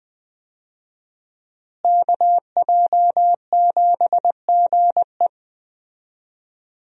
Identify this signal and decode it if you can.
Morse code, heard as a keyed tone, about 20 wpm, 700 Hz. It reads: KJ7GE
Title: USA Call Signs - 5 Characters - Repeated 3x - 20wpm